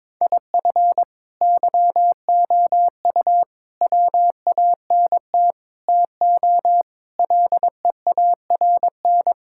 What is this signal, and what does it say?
Morse code, heard as a keyed tone, about 22 wpm, 700 Hz. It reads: IF YOU WANT TO LEARN
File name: practice / 22wpm / Sets of 7 Words - 22wpm